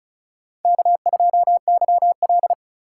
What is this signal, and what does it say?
Morse code, heard as a keyed tone, about 34 wpm, 700 Hz. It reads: K2YL
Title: USA Call Signs - 2x Character Spacing - 35wpm